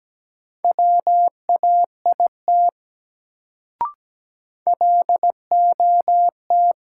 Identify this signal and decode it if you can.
Morse code, heard as a keyed tone, about 17 wpm, 700 Hz. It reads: WAIT LOT
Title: Sentences from Top 500 Words - New Word Drill - 17wpm